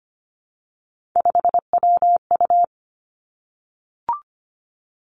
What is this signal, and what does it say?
Morse code, heard as a keyed tone, about 25 wpm, 700 Hz. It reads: 5WU
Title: Three Challenging Letter-Number Combinations 25wpm